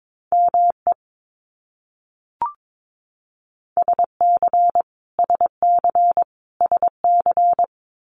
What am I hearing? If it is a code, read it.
Morse code, heard as a keyed tone, about 22 wpm, 700 Hz. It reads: ME SC SC SC
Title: US States Repeated 3x 22wpm